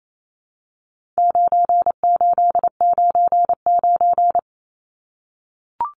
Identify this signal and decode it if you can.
Morse code, heard as a keyed tone, about 28 wpm, 700 Hz. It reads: 9899
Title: Numbers - 4-Digit 28wpm